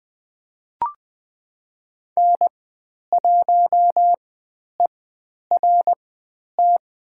Morse code, heard as a keyed tone, about 20 wpm, spaced as for 10 wpm, 700 Hz. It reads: N1ERT
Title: 102 Lesson 17g - Letter u - Call Signs 20wpm